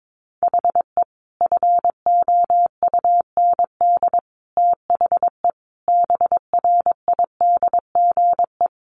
Morse code, heard as a keyed tone, about 22 wpm, 700 Hz. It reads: HE FOUND THE BRIDGE
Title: Sentences from Top 1200 Words 22wpm